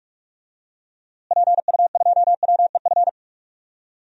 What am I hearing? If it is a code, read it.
Morse code, heard as a keyed tone, about 46 wpm, 700 Hz. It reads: PU2WEF
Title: DXCC-100 Call Signs - 6 Characters - Repeated 3x - 45wpm